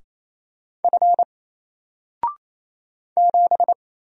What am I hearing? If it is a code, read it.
Morse code, heard as a keyed tone, about 27 wpm, 700 Hz. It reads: F 7
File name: Single-Letter-Number-28wpm